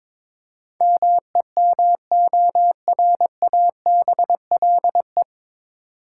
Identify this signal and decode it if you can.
Morse code, heard as a keyed tone, about 21 wpm, 700 Hz. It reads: MEMORABLE
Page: Morse Code Ninja practice 22wpm